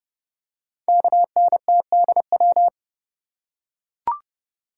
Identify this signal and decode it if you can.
Morse code, heard as a keyed tone, about 30 wpm, 690 Hz. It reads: KNTDW
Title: Code Groups - 5 Letters - 30wpm